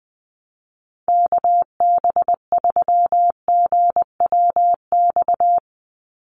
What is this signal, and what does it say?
Morse code, heard as a keyed tone, about 20 wpm, 700 Hz. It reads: KB3GWX
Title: USA Call Signs 20wpm